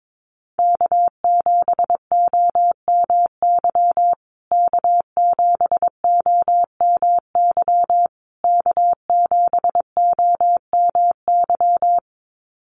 Morse code, heard as a keyed tone, about 22 wpm, 700 Hz. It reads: K7OMY K7OMY K7OMY
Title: USA Call Signs Repeated 3x 22wpm